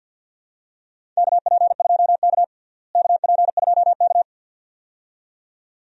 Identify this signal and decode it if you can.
Morse code, heard as a keyed tone, about 53 wpm, 700 Hz. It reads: KP2K KP2K
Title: DXCC-100 Call Signs - 4 Characters - Repeated 2x - 50wpm